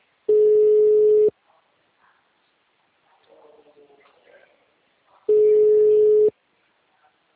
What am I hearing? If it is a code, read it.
Ringback tone (Europe, 425 Hz), with speech.
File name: out- -2035- - - .87180